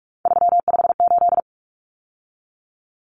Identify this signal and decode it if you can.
Morse code, heard as a keyed tone, about 44 wpm, 700 Hz. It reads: 358
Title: Numbers - 3-Digit 45wpm